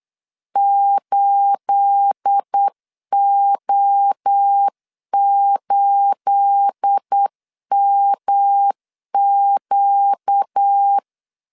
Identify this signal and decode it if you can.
Morse code, heard as a keyed tone, about 8 wpm, 800 Hz. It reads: 8O8MQ
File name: CW / DET / 1 9 1500C S A 8wpm 5637